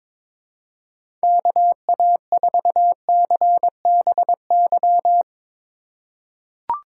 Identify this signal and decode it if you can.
Morse code, heard as a keyed tone, about 22 wpm, 700 Hz. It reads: KA4CBY
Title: USA Call Signs - 6 Characters - 2x Character Spacing - 22wpm